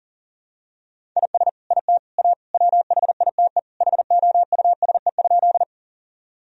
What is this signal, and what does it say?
Morse code, heard as a keyed tone, about 40 wpm, 700 Hz. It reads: IS IT A WHITE HOUSE?
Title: Sentences from Top 1000 Words - Review Part 1 - 40wpm